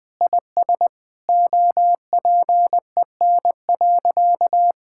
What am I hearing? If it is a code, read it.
Morse code, heard as a keyed tone, about 21 wpm, 700 Hz. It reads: IS OPEN.
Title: Sentences from Top 1000 Words - Review Part 6 - 20wpm